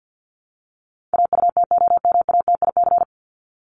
Morse code, heard as a keyed tone, about 50 wpm, 700 Hz. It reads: AUTOMATIC
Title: Top 1751-2000 Words 50wpm